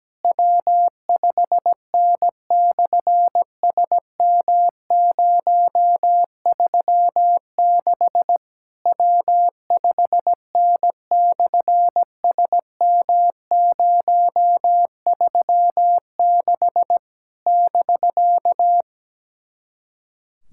Morse code, heard as a keyed tone, about 17 wpm, 700 Hz. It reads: W5N/SM036 W5N/SM036 <BK>
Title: SOTA QSO Practice 17wpm